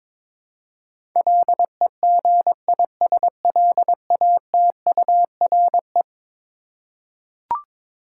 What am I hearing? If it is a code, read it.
Morse code, heard as a keyed tone, about 22 wpm, 700 Hz. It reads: LEGISLATURE
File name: practice / 22wpm / Eleven Letter Words - 22wpm